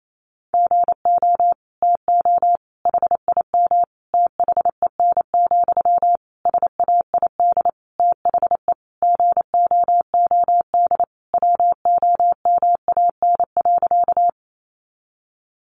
Morse code, heard as a keyed tone, about 28 wpm, 700 Hz. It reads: GO TO HIM THEN, SAID THE GOOD WOMAN.
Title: Sentences from Top 1100 Words 28wpm